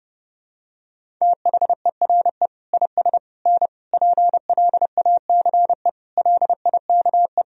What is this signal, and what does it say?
Morse code, heard as a keyed tone, about 32 wpm, 700 Hz. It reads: THERE IS N PLACE LIKE
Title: Sentences from Top 200 Words and Abbreviations 30wpm